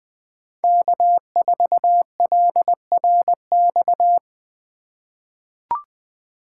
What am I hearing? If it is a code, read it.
Morse code, heard as a keyed tone, about 20 wpm, 700 Hz. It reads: K4LRX
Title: K1USN SST QSO Practice - 20wpm